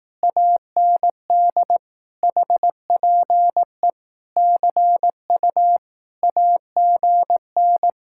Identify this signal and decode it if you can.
Morse code, heard as a keyed tone, about 18 wpm, 700 Hz. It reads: AND HPE CU AGN
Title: Speed-Racing Common QSO Phrases Encore - 15wpm